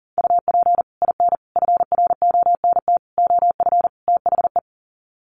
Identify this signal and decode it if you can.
Morse code, heard as a keyed tone, about 40 wpm, 700 Hz. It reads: UP IN FRONT OF THE
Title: Sets of 5 Words Encore2 40wpm